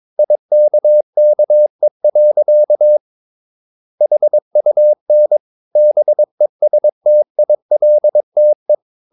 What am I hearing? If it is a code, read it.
Morse code, heard as a keyed tone, about 22 wpm, 600 Hz. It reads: IKKE. HUN BESTILTE
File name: Ferie QRS 22wpm